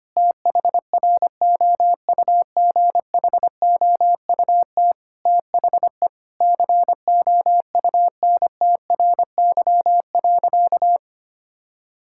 Morse code, heard as a keyed tone, about 25 wpm, 700 Hz. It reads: THROUGHOUT THE COUNTRY.
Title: Sentences from Top 1000 Words - Review Part 5 - 25wpm